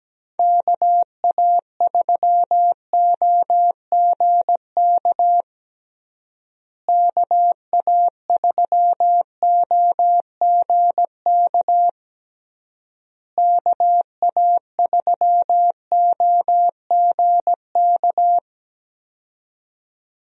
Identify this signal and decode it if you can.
Morse code, heard as a keyed tone, about 17 wpm, 700 Hz. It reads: KA3OGK KA3OGK KA3OGK
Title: USA Call Signs - 6 Characters - Repeated 3x - 17wpm